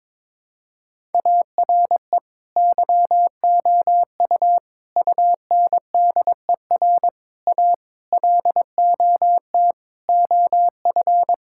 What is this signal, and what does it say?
Morse code, heard as a keyed tone, about 22 wpm, 700 Hz. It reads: ARE YOU UNDER A LOT OF
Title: Sentences from Top 1000 Words - Review Part 6 - 22wpm